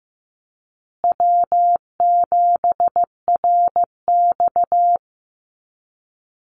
Morse code, heard as a keyed tone, about 15 wpm, 700 Hz. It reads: W7RX
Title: Call Signs Repeated 2x - 15wpm